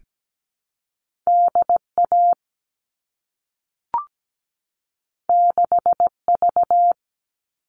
Morse code, heard as a keyed tone, about 17 wpm, 700 Hz. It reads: DA 6V
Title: All Cut Numbers - 2-Digit 17wpm